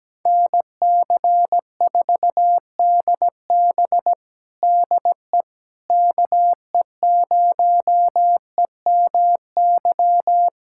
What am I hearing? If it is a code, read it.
Morse code, heard as a keyed tone, about 17 wpm, 700 Hz. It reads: NC4DB DE KE0EMY
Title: USA Callsign DE Callsign 17wpm